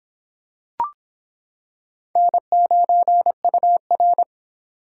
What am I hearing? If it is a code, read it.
Morse code, heard as a keyed tone, about 26 wpm, 700 Hz. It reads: N9UR